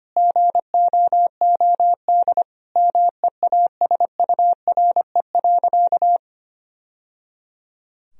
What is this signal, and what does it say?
Morse code, heard as a keyed tone, about 25 wpm, 700 Hz. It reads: GOOD MEASURE.
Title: Sentences from Top 1000 Words - Review Part 2 - 25wpm